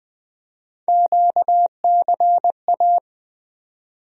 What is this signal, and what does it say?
Morse code, heard as a keyed tone, about 20 wpm, 700 Hz. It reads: QCA
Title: USA Call Sign Postfixes - 3 Letters 20wpm